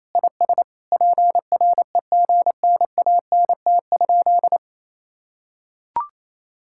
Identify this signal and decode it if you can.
Morse code, heard as a keyed tone, about 27 wpm, 700 Hz. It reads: IS PREGNANT?
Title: Sentences from Top 1000 Words - Review Part 6 - 28wpm